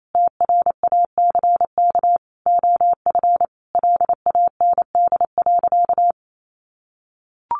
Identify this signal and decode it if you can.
Morse code, heard as a keyed tone, about 28 wpm, 700 Hz. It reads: TRACK OF LAND.
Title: Sentences from Top 1300 Words 28wpm